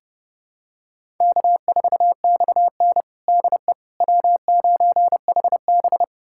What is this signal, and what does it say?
Morse code, heard as a keyed tone, about 30 wpm, 700 Hz. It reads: K4XN DE W9HB
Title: USA Callsign DE Callsign - 4 Characters - 30wpm